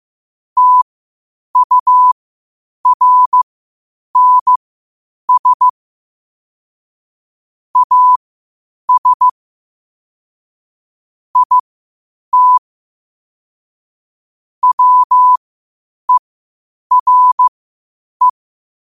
Morse code, heard as a keyed tone, about 15 wpm, spaced as for 8 wpm, 1000 Hz. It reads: TURNS AS IT WERE